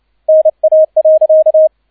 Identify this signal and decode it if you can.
Morse code, heard as a keyed tone, about 29 wpm, 630 Hz, with radio noise. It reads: NA.
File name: 002 Zweiter Streich 30wpm